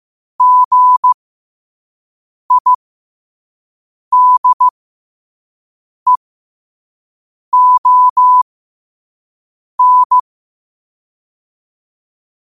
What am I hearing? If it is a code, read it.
Morse code, heard as a keyed tone, about 15 wpm, spaced as for 5 wpm, 1000 Hz. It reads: GIDEON
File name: 5wpm